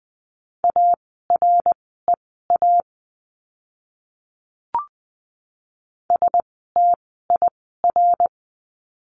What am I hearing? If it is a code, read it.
Morse code, heard as a keyed tone, about 20 wpm, spaced as for 14 wpm, 700 Hz. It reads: AREA STIR
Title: 061 Lesson 11f - Letter h - Words 20wpm